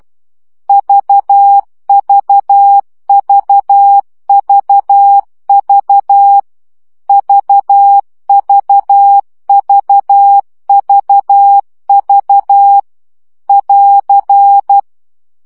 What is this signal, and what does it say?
Morse code, heard as a keyed tone, about 12 wpm, 800 Hz. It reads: VVVVV VVVVV <AR>
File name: CW gyakorlas v 10 baud